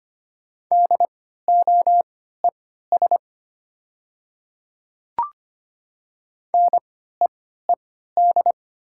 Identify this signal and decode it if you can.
Morse code, heard as a keyed tone, about 25 wpm, spaced as for 14 wpm, 700 Hz. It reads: DOES NEED
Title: 3x Character Spacing - Top 300 Words - 25wpm